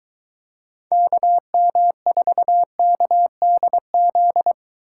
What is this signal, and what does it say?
Morse code, heard as a keyed tone, about 22 wpm, 700 Hz. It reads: KM4KDZ